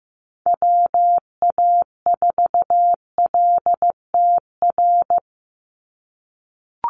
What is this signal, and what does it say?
Morse code, heard as a keyed tone, about 15 wpm, 700 Hz. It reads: WA4LTR
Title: USA Call Signs 15wpm